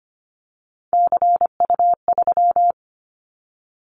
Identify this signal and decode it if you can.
Morse code, heard as a keyed tone, about 25 wpm, 700 Hz. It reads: CU3
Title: Callsign Prefixes for Top 100 DXCC - 25wpm